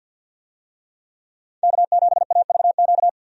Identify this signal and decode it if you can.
Morse code, heard as a keyed tone, about 50 wpm, 700 Hz. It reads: K7AVQ
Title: USA Call Signs - 5 Characters - 50wpm